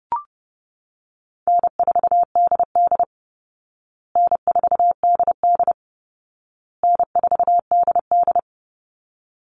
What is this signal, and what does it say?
Morse code, heard as a keyed tone, about 30 wpm, 700 Hz. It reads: N4DD N4DD N4DD
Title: Easy Call Signs Repeated 3x - 30wpm